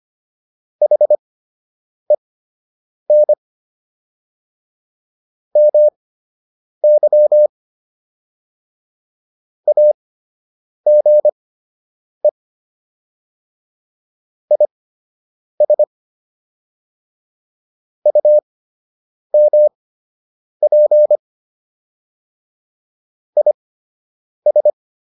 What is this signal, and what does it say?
Morse code, heard as a keyed tone, about 25 wpm, spaced as for 8 wpm, 600 Hz. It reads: HEN MY AGE IS UMP IS